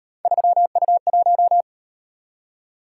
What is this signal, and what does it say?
Morse code, heard as a keyed tone, about 38 wpm, 700 Hz. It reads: 3U1